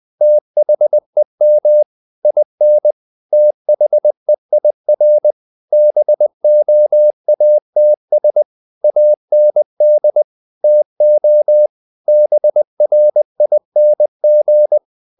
Morse code, heard as a keyed tone, about 20 wpm, 600 Hz. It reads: THEM IN THEIR BOATS AND TO BRING
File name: the old man and the sea / the old man and the sea part 1 20wpm